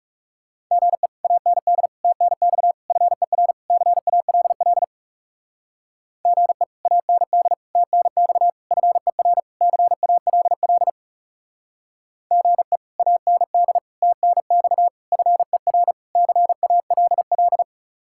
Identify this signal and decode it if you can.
Morse code, heard as a keyed tone, about 38 wpm, 700 Hz. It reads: GE AND TNX FER CALL GE AND TNX FER CALL GE AND TNX FER CALL